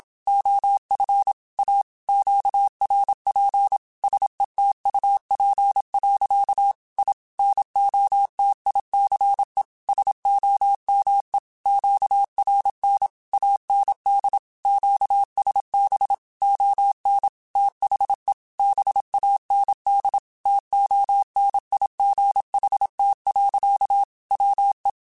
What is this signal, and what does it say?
Morse code, heard as a keyed tone, about 27 wpm, 800 Hz. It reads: OF A QRP SETUP. I NOTICE SOME QRN AND QSB ON THE BAND TONIGHT. WE
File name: QSOs / QSOs-mp3s / WD0FBT 1@25